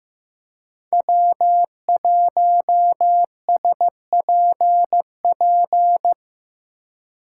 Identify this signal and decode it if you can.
Morse code, heard as a keyed tone, about 15 wpm, 700 Hz. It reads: W1SPP
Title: USA Call Signs - 2x Character Spacing - 15wpm